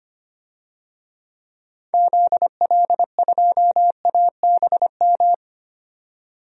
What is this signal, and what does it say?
Morse code, heard as a keyed tone, about 25 wpm, 700 Hz. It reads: ZL2ABM
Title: DXCC-100 Call Signs - 6 Characters - 25wpm